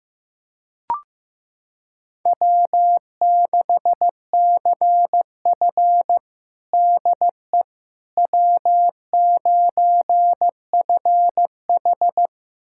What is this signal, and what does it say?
Morse code, heard as a keyed tone, about 15 wpm, 700 Hz. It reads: W6CF DE W9FH
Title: USA Callsign DE Callsign - 4 Characters - 15wpm